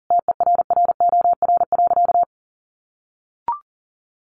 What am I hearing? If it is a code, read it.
Morse code, heard as a keyed tone, about 40 wpm, 700 Hz. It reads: TERROR.
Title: A Sound Of Thunder 40wpm